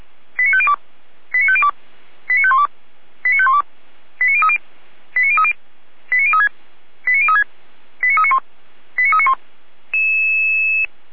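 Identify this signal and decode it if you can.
Five-tone selective call (ZVEI), heard as quick runs of five tones, tones of about 70 ms. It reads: 89502 89502 89512 89512 89030 89030 89036 89036 89391 89391
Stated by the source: ZVEI five-tone selective call received on an SDR.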